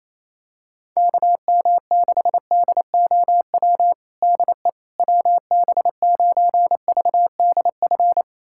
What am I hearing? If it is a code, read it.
Morse code, heard as a keyed tone, about 28 wpm, 700 Hz. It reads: KM6DOW DE WB9VDF
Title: USA Callsign DE Callsign - 6 Characters - 28wpm